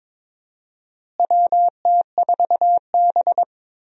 Morse code, heard as a keyed tone, about 22 wpm, 700 Hz. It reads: WT4B
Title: USA Call Signs - 4 Characters - Repeated 3x - 22wpm